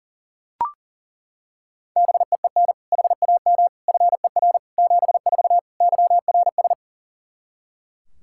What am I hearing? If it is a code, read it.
Morse code, heard as a keyed tone, about 37 wpm, 700 Hz. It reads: BEEN HAM FER 74 YRS
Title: K1USN SST QSO Practice 40wpm